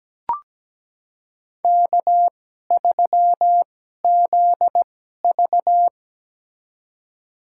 Morse code, heard as a keyed tone, about 17 wpm, spaced as for 12 wpm, 700 Hz. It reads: K3ZV
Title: USA Call Signs - 4 Characters - 2x Character Spacing - 17wpm